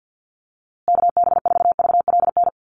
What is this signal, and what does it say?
Morse code, heard as a keyed tone, about 50 wpm, 700 Hz. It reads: KB4VLN
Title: USA Call Signs Repeated 2x 50wpm